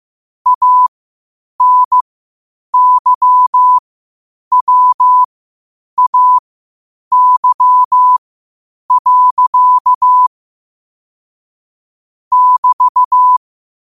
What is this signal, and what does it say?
Morse code, heard as a keyed tone, about 15 wpm, spaced as for 8 wpm, 1000 Hz. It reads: ANYWAY. <BT>
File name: qotdfiles / 8wpm